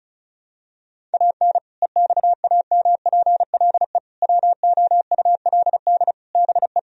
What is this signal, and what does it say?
Morse code, heard as a keyed tone, about 36 wpm, 700 Hz. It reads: AN EXAMPLE WOULD BE
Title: Sentences from Top 1000 Words - Review Part 1 - 35wpm